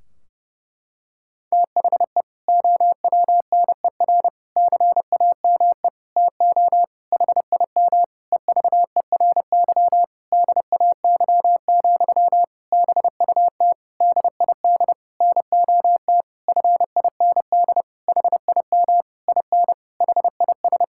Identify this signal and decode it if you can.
Morse code, heard as a keyed tone, about 30 wpm, 700 Hz. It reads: THE OWNER CAME TO HIM EVERY DAY, BUT DID NOT FIND HIM IN HIS
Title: Sentences from Top 1600 Words 30wpm